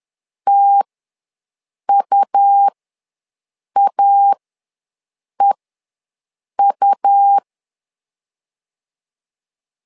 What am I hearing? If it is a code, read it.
Morse code, heard as a keyed tone, about 11 wpm, spaced as for 6 wpm, 800 Hz. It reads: TUAEU
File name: LEZIONI / 2 UHDN / 2 U ws 10wpm 2017